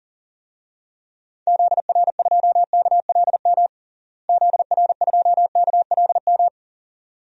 Morse code, heard as a keyed tone, about 40 wpm, 700 Hz. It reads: ZR2KLM ZR2KLM
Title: DXCC-100 Call Signs - 6 Characters - Repeated 2x - 40wpm